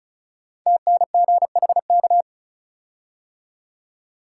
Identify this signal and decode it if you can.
Morse code, heard as a keyed tone, about 36 wpm, 700 Hz. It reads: TNGHK
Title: Code Groups - 5 Letter:Numbers - 35wpm